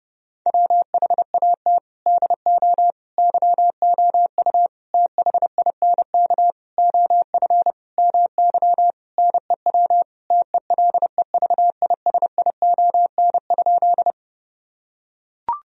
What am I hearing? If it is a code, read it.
Morse code, heard as a keyed tone, about 30 wpm, 700 Hz. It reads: WHAT DO YOU THINK OF MY NEW TELEVISION?
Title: Sentences from Top 1000 Words - Review Part 4 - 30wpm